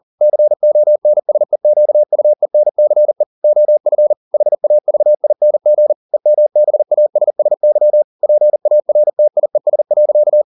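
Morse code, heard as a keyed tone, about 40 wpm, 600 Hz. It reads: CONSEQUENCE OF HAVING EMBASSY PARTIES.
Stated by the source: Morse code at 40 wpm, practice text, CW med Gnister.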